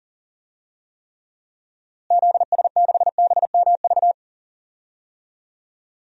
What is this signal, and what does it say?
Morse code, heard as a keyed tone, about 40 wpm, 700 Hz. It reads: ZS6BMV
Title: DXCC-100 Call Signs - 6 Characters - Repeated 3x - 40wpm